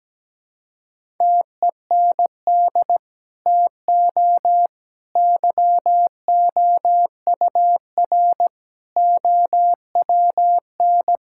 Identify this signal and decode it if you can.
Morse code, heard as a keyed tone, about 17 wpm, 700 Hz. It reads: TEND TO YOUR OWN
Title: Sentences from Top 1000 Words - Review Part 5 - 17wpm